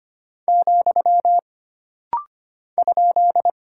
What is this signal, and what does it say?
Morse code, heard as a keyed tone, about 25 wpm, 700 Hz. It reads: , ?
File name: Single Punctuation-Prosign - Rapid-Fire - 25wpm